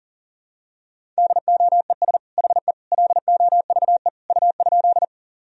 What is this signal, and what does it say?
Morse code, heard as a keyed tone, about 40 wpm, 700 Hz. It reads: DOES HE LOVE U?
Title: Sentences from Top 400 Words and Abbreviations 40wpm